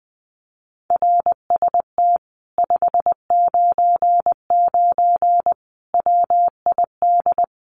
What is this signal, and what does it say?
Morse code, heard as a keyed tone, about 19 wpm, 700 Hz. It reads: RST 599 WID
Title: Common QSO Phrases Encore 20wpm